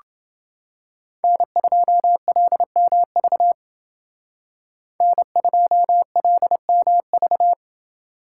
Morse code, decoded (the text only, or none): N2LMV N2LMV